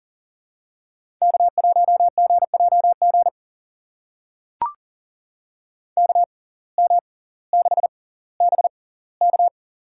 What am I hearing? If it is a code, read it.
Morse code, heard as a keyed tone, about 40 wpm, spaced as for 20 wpm, 700 Hz. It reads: K1GJG KM6BK
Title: USA Call Signs - 5 Characters - 6x Character Spacing - 40wpm